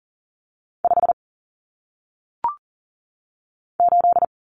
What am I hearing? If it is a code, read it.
Morse code, heard as a keyed tone, about 39 wpm, 700 Hz. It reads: 5 8
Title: Numbers - 1-Digit 40wpm